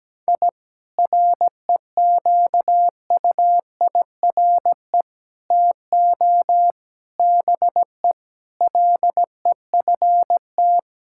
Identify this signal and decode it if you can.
Morse code, heard as a keyed tone, about 17 wpm, 700 Hz. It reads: I REQUIRE TO BE LEFT
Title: Sentences from Top 1000 Words - Review Part 6 - 17wpm